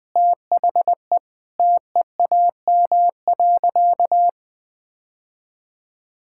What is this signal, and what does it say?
Morse code, heard as a keyed tone, about 20 wpm, 700 Hz. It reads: THE TEAM.
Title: Sentences with 3-Word Phrases - Spoken Beginning - 20wpm